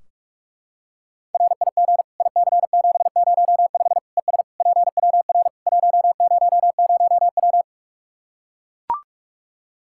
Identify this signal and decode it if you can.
Morse code, heard as a keyed tone, about 44 wpm, 700 Hz. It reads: RIG IC705 ES PWR 100W
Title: Practice QSO - Exchange 2 - Info - 45wpm